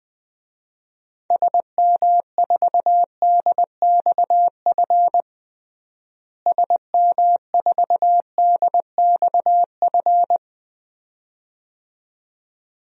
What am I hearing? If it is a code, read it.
Morse code, heard as a keyed tone, about 20 wpm, 700 Hz. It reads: SM4DXF SM4DXF
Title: DXCC-100 Call Signs - 6 Characters - Repeated 2x - 20wpm